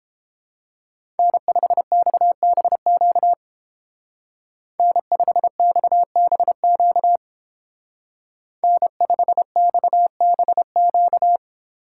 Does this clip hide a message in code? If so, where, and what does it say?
Yes: Morse code from 1.2 to 11.4 s, heard as a keyed tone, about 29 wpm, 700 Hz. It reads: N5XBQ N5XBQ N5XBQ